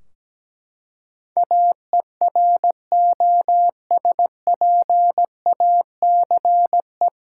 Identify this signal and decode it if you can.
Morse code, heard as a keyed tone, about 17 wpm, 700 Hz. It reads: AEROSPACE
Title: Practice QSO - Exchange 3 - More Info - 17wpm